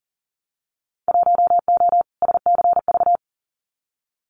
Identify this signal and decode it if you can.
Morse code, heard as a keyed tone, about 38 wpm, 700 Hz. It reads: 1O SCV
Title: Winter Field Day Exchanges 40wpm